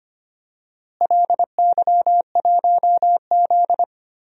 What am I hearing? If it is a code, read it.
Morse code, heard as a keyed tone, about 25 wpm, 700 Hz. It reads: LY1Z
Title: DXCC-100 Call Signs - 4 Characters - Repeated 3x - 25wpm